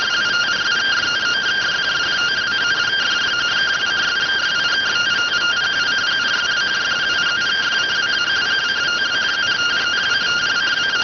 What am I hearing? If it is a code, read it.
Two-tone teleprinter signal, about 170 Hz shift.